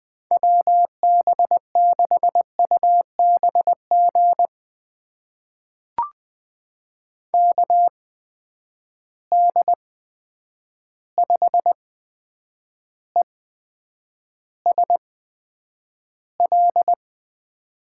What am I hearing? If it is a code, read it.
Morse code, heard as a keyed tone, about 20 wpm, spaced as for 9 wpm, 700 Hz. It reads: WB6UBG KD5ESL
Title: USA Call Signs - 6 Characters - 8x Character Spacing - 20wpm